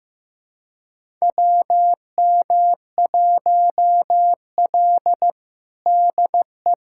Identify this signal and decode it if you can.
Morse code, heard as a keyed tone, about 16 wpm, 700 Hz. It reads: WM1L DE
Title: USA Callsign DE Callsign - 4 Characters - 15wpm